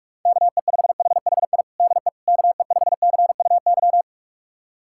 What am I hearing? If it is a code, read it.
Morse code, heard as a keyed tone, about 46 wpm, 690 Hz. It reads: KE5HHI DE KE5CUY
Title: USA Callsign DE Callsign - 6 Characters - 45wpm